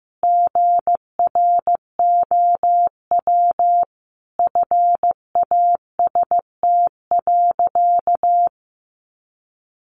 Morse code, heard as a keyed tone, about 15 wpm, 700 Hz. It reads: GROW FAST.
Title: Sentences from Top 1300 Words 15wpm